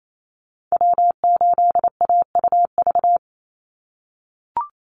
Morse code, heard as a keyed tone, about 28 wpm, 700 Hz. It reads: W8AUV
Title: USA Call Signs 28wpm